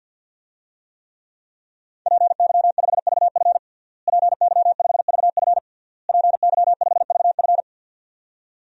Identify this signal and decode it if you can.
Morse code, heard as a keyed tone, about 47 wpm, 700 Hz. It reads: PY5VF PY5VF PY5VF
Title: DXCC-100 Call Signs - 5 Characters - Repeated 3x - 50wpm